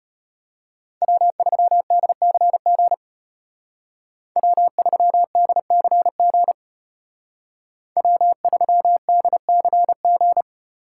Morse code, heard as a keyed tone, about 33 wpm, 700 Hz. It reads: W3DCG W3DCG W3DCG